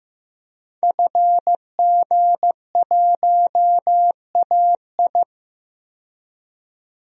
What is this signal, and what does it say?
Morse code, heard as a keyed tone, about 15 wpm, 700 Hz. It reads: FG1AI
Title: DXCC-100 Call Signs - 5 Characters - 15wpm